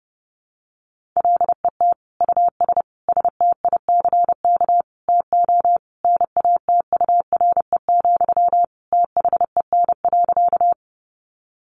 Morse code, heard as a keyed tone, about 30 wpm, 700 Hz. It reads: LET US STICK TO NATURE, THEN.
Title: Sentences from Top 1200 Words 30wpm